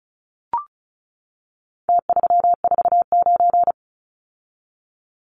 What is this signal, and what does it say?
Morse code, heard as a keyed tone, about 34 wpm, 700 Hz. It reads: T349
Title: Numbers - Contest Style - 4-Digit 35wpm